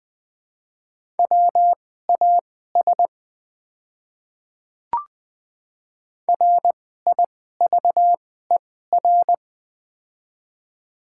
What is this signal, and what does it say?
Morse code, heard as a keyed tone, about 20 wpm, spaced as for 14 wpm, 700 Hz. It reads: WAS RIVER
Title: 2x Character Spacing - Top 300 Words - 20wpm